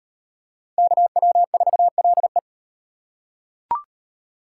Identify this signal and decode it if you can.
Morse code, heard as a keyed tone, about 34 wpm, 700 Hz. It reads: KW4LE